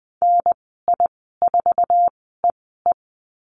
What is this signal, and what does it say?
Morse code, heard as a keyed tone, about 20 wpm, spaced as for 15 wpm, 700 Hz. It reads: NI4EE
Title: 069 Lesson 12g - Letter d - Call Signs 20wpm